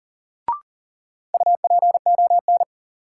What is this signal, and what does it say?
Morse code, heard as a keyed tone, about 40 wpm, 700 Hz. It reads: UPON
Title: Sentences from Top 1100 Words - New Word Drill Alt - 40wpm